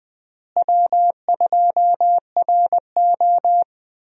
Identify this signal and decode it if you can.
Morse code, heard as a keyed tone, about 20 wpm, 700 Hz. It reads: W2RO
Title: USA Call Signs - 4 Characters - Repeated 3x - 20wpm